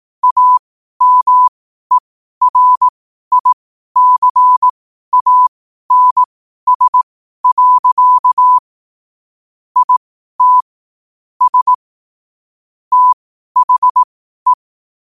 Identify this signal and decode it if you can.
Morse code, heard as a keyed tone, about 18 wpm, spaced as for 12 wpm, 1000 Hz. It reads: AMERICANS. IT S THE
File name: qotdfiles / 12wpm